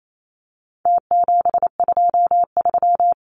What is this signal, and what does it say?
Morse code, heard as a keyed tone, about 28 wpm, 700 Hz. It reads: T723
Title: Numbers - Contest Style - 4-Digit 28wpm